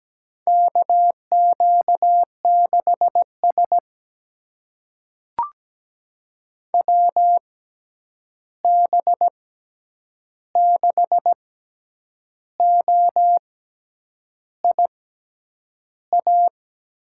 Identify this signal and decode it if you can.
Morse code, heard as a keyed tone, about 17 wpm, spaced as for 8 wpm, 700 Hz. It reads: KQ6S WB6OIA
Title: USA Call Signs - 6x Character Spacing - 17wpm